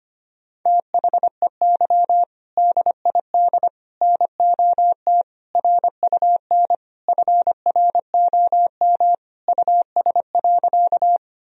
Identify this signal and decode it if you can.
Morse code, heard as a keyed tone, about 25 wpm, 700 Hz. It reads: THEY DID NOT RUN FROM US.
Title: Sentences from Top 1000 Words - Review Part 1 - 25wpm